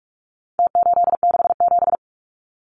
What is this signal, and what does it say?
Morse code, heard as a keyed tone, about 44 wpm, 700 Hz. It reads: T867
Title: Numbers - Contest Style - 4-Digit 45wpm